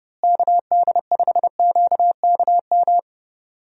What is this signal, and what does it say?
Morse code, heard as a keyed tone, about 30 wpm, 700 Hz. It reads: KD5QKM
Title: USA Call Signs - 6 Characters - Repeated 3x - 30wpm